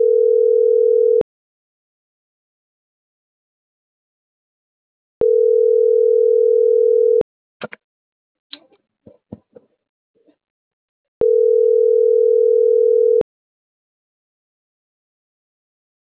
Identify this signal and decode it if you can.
Ringback tone (North America), 440 + 480 Hz.